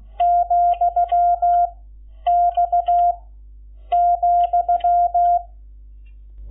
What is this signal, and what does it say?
Morse code, heard as a keyed tone, about 16 wpm, 700 Hz. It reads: , X ,
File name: Lecciones / CLASE Nº11